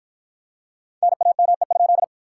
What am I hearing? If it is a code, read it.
Morse code, heard as a keyed tone, about 48 wpm, 700 Hz. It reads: NAME?